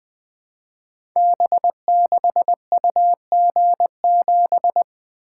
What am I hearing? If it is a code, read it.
Morse code, heard as a keyed tone, about 20 wpm, 700 Hz. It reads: B6UG7
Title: Code Groups - 5 Letter:Numbers - 20wpm